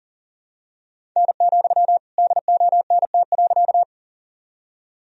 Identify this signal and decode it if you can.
Morse code, heard as a keyed tone, about 40 wpm, 700 Hz. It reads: N, DONT.
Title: Sentences from Top 300 Words and Abbreviations 40wpm